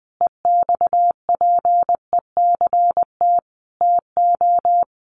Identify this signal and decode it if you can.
Morse code, heard as a keyed tone, about 20 wpm, 700 Hz. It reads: EXPECT TO
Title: Sentences from Top 1100 Words 20wpm